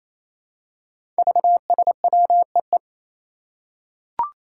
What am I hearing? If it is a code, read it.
Morse code, heard as a keyed tone, about 28 wpm, 700 Hz. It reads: VSWEE
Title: Five Challenging Letter-Number Combinations 28wpm